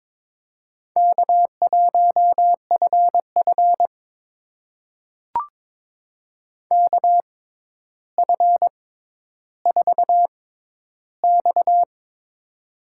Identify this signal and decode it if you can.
Morse code, heard as a keyed tone, about 23 wpm, spaced as for 11 wpm, 700 Hz. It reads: K1FF KF4X
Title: USA Call Signs - 4 Characters - 6x Character Spacing - 22wpm